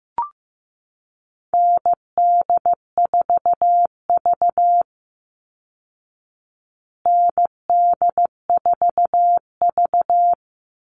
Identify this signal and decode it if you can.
Morse code, heard as a keyed tone, about 15 wpm, 700 Hz. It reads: ND4V ND4V
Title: Call Signs Repeated 3x - 15wpm